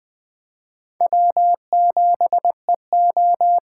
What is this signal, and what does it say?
Morse code, heard as a keyed tone, about 20 wpm, 700 Hz. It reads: W7EO
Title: Speed Racing - USA 1x2 Call Signs 17wpm